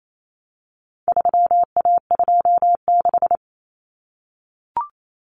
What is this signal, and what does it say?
Morse code, heard as a keyed tone, about 28 wpm, 700 Hz. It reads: 3A26
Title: Common Cut Numbers - 4-Digit 28wpm